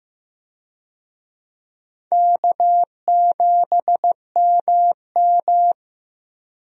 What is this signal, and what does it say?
Morse code, heard as a keyed tone, about 15 wpm, 700 Hz. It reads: K7MM
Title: USA Call Signs - 4 Characters - 15wpm